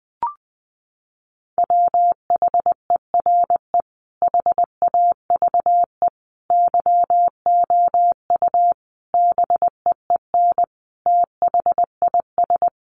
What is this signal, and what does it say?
Morse code, heard as a keyed tone, about 20 wpm, 700 Hz. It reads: WHERE HAVE YOU BEEN THIS
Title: Sentences from Top 300 Words 20wpm